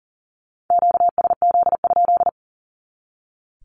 Morse code, heard as a keyed tone, about 40 wpm, 700 Hz. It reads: QSZ?
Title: Q-Codes 40wpm